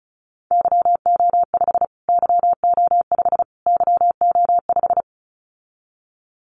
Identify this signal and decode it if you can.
Morse code, heard as a keyed tone, about 34 wpm, 700 Hz. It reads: YO5 YO5 YO5
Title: Callsign Prefixes for Top 25 DXCC - Repeated 3x - 35wpm